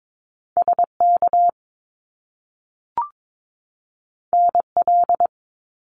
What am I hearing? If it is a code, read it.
Morse code, heard as a keyed tone, about 22 wpm, 700 Hz. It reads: SK NL
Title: Canadian Provinces and Territories 22wpm